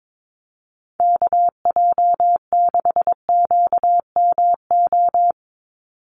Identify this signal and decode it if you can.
Morse code, heard as a keyed tone, about 22 wpm, 700 Hz. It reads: KJ6QMO
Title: USA Call Signs 22wpm